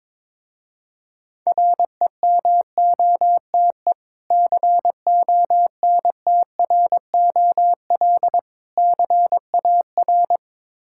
Morse code, heard as a keyed tone, about 22 wpm, 690 Hz. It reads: REMOTE CONTROL CAR
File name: Difficult Words with Hints - 22wpm